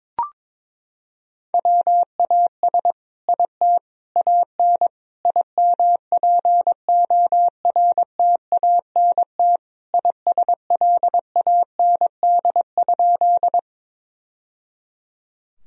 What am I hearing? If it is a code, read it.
Morse code, heard as a keyed tone, about 22 wpm, 700 Hz. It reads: WAS IT AN IMPORTANT ISLAND?
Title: Sentences from Top 1000 Words - Review Part 3 - 22wpm